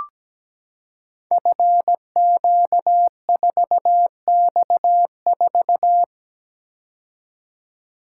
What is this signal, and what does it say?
Morse code, heard as a keyed tone, about 17 wpm, 700 Hz. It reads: FQ4X4
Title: Code Groups - 5 Letter:Numbers - 17wpm